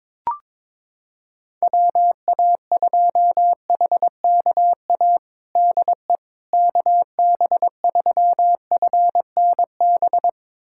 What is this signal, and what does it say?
Morse code, heard as a keyed tone, about 22 wpm, 700 Hz. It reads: WA2HKA DE KB3FNB
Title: USA Callsign DE Callsign - 6 Characters - 22wpm